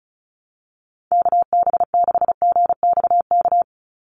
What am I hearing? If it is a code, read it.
Morse code, heard as a keyed tone, about 34 wpm, 700 Hz. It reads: KB6GXK
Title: USA Call Signs 35wpm